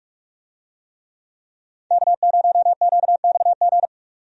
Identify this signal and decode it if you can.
Morse code, heard as a keyed tone, about 43 wpm, 700 Hz. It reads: K0QXG
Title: USA Call Signs - 5 Characters - 45wpm